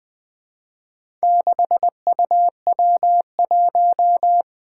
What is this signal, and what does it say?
Morse code, heard as a keyed tone, about 20 wpm, 700 Hz. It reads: 6UW1
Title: Four Challenging Letter-Number Combinations 20wpm